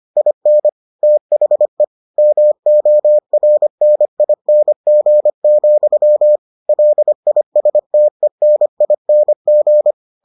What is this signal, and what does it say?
Morse code, heard as a keyed tone, about 25 wpm, 600 Hz. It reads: IN THE MORNING, LISTENING